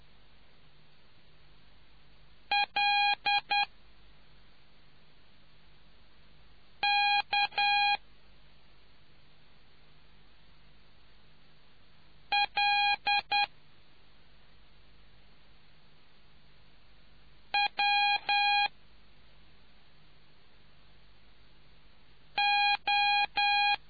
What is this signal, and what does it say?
Morse code, heard as a keyed tone, about 10 wpm, spaced as for 2 wpm, 820 Hz. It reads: LKLWO